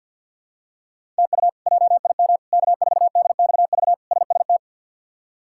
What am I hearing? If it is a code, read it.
Morse code, heard as a keyed tone, about 50 wpm, 700 Hz. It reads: TU JIM K4DXV SST
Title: K1USN SST QSO Practice - 50wpm